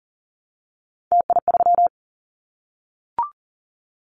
Morse code, heard as a keyed tone, about 38 wpm, 700 Hz. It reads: TI3
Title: Callsign Prefixes for Top 100 DXCC - 40wpm